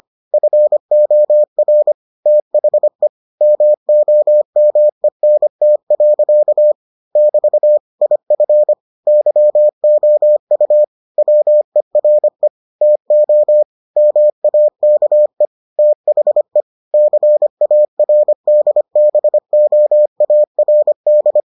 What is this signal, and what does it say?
Morse code, heard as a keyed tone, about 25 wpm, 600 Hz. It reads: FOR THE MOMENT. <BT> IF YOU WERE TO MAKE THE CARDBOARD